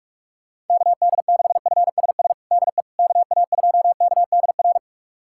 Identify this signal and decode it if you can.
Morse code, heard as a keyed tone, about 46 wpm, 700 Hz. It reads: KD6FSS DE KA2KDR
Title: USA Callsign DE Callsign 45wpm